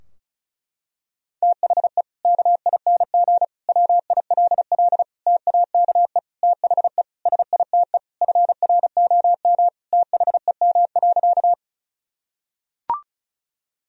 Morse code, heard as a keyed tone, about 36 wpm, 700 Hz. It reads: THE KING WILL TAKE THE SITE FROM THEM.
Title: Sentences from Top 1000 Words - Review Part 4 - 35wpm